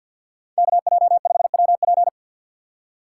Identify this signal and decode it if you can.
Morse code, heard as a keyed tone, about 52 wpm, 690 Hz. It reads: KJ5WP
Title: USA Call Signs - 5 Characters - 6x Character Spacing - 50wpm